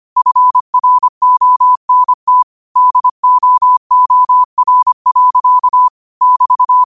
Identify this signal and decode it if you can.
Morse code, heard as a keyed tone, about 25 wpm, 1000 Hz. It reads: FRONT DOOR. <BT>
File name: qotdfiles / 25wpm